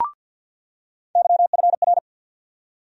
Morse code, heard as a keyed tone, about 48 wpm, 700 Hz. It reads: YFR
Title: USA Call Sign Postfixes - 3 Letters 50wpm